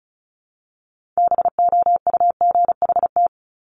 Morse code, heard as a keyed tone, about 36 wpm, 700 Hz. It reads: BOUGHT
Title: Multi-letter Phonemes Repeated 3x 35wpm